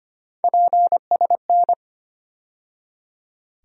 Morse code, heard as a keyed tone, about 25 wpm, 700 Hz. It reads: PSN
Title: USA Call Sign Postfixes - 3 Letters 25wpm